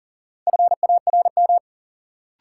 Morse code, heard as a keyed tone, about 40 wpm, 700 Hz. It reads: FARM
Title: Sentences from Top 1000 Words - Review Part 2 - New Word Drill - 40wpm